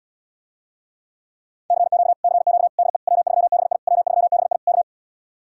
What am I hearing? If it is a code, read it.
Morse code, heard as a keyed tone, about 80 wpm, 700 Hz. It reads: CQ CQ DE W3BI W3BI K